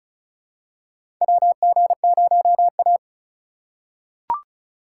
Morse code, heard as a keyed tone, about 36 wpm, 700 Hz. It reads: WG0A
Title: USA Call Signs - 3x Character Spacing - 35wpm